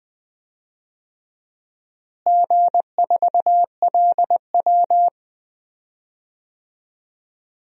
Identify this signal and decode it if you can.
Morse code, heard as a keyed tone, about 20 wpm, 700 Hz. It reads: G4LW
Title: DXCC-100 Call Signs - 4 Characters - Repeated 2x - 20wpm